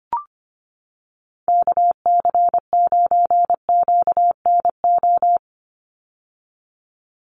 Morse code, heard as a keyed tone, about 25 wpm, 690 Hz. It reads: KC9QNO
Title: USA Call Signs 25wpm